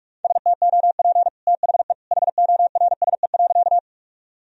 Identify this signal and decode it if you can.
Morse code, heard as a keyed tone, about 46 wpm, 700 Hz. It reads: STOP THE HORSE.
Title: Sentences from Top 1000 Words - Review Part 1 - 45wpm